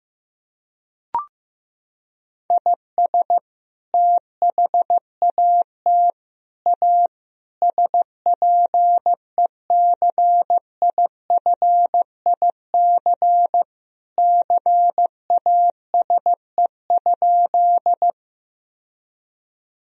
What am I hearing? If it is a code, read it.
Morse code, heard as a keyed tone, about 15 wpm, 700 Hz. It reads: IS THAT A SPECIFIC CASE?
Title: Sentences from Top 1000 Words - Review Part 3 - 15wpm